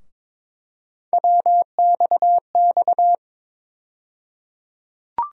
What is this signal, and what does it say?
Morse code, heard as a keyed tone, about 22 wpm, 700 Hz. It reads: WXX
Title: USA Call Sign Postfixes - 3 Letters 22wpm